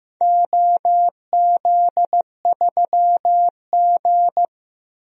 Morse code, heard as a keyed tone, about 15 wpm, 700 Hz. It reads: OZ3G
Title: DXCC-100 Call Signs - 4 Characters - Repeated 2x - 15wpm